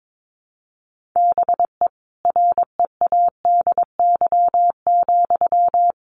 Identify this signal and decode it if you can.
Morse code, heard as a keyed tone, about 22 wpm, 700 Hz. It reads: BE READY,
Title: Sentences from Top 1400 Words 22wpm